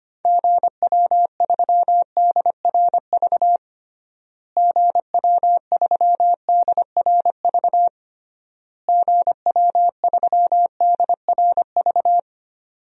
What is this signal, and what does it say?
Morse code, heard as a keyed tone, about 25 wpm, 700 Hz. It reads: GW3DRV GW3DRV GW3DRV
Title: DXCC-100 Call Signs - 6 Characters - Repeated 3x - 25wpm